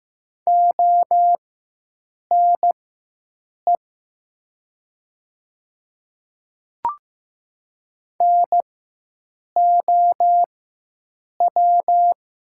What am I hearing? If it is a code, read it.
Morse code, heard as a keyed tone, about 15 wpm, spaced as for 7 wpm, 700 Hz. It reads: ONE NOW
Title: 4x Character Spacing - Top 100 Words - 15wpm